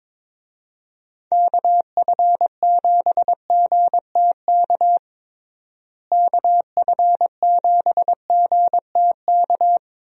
Morse code, heard as a keyed tone, about 22 wpm, 700 Hz. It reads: KF7GTK KF7GTK
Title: USA Call Signs - 6 Characters - Repeated 2x - 22wpm